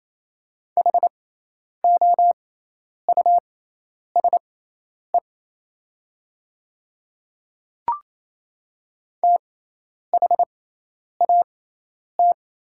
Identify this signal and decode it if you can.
Morse code, heard as a keyed tone, about 27 wpm, spaced as for 10 wpm, 700 Hz. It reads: HOUSE THAT
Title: 6x Character Spacing - Top 200 Words - 28wpm